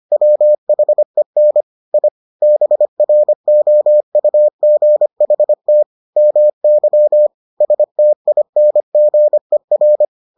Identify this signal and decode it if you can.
Morse code, heard as a keyed tone, about 25 wpm, 600 Hz. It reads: WHEN I BROUGHT MY STINGER